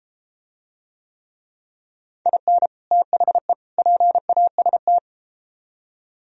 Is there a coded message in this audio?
Yes: Morse code from 2.3 to 5.0 s, heard as a keyed tone, about 33 wpm, 700 Hz. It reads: IN THE PAST